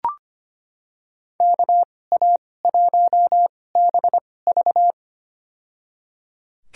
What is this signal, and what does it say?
Morse code, heard as a keyed tone, about 25 wpm, spaced as for 18 wpm, 700 Hz. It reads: KA1BV
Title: USA Call Signs - 5 Characters - 2x Character Spacing - 25wpm